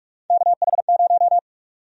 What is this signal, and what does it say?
Morse code, heard as a keyed tone, about 46 wpm, 700 Hz. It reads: KH0
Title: USA Call Sign Prefixes - Repeated 2x - 45wpm